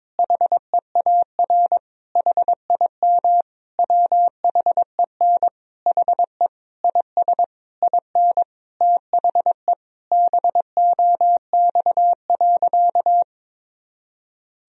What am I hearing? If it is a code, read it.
Morse code, heard as a keyed tone, about 22 wpm, 700 Hz. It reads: HEAR HIM WHEN HE IS IN THE BOX.
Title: Sentences from Top 1000 Words - Review Part 2 - 22wpm